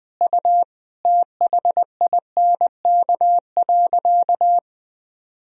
Morse code, heard as a keyed tone, about 20 wpm, 700 Hz. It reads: U THINK.
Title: Sentences from Top 400 Words and Abbreviations 20wpm